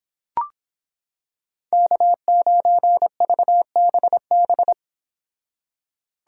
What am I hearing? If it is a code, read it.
Morse code, heard as a keyed tone, about 26 wpm, 700 Hz. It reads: K9VBB